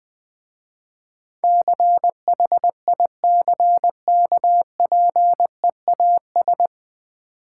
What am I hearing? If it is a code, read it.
Morse code, heard as a keyed tone, about 21 wpm, 700 Hz. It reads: CHICKPEAS
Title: Difficult Words with Hints - 20wpm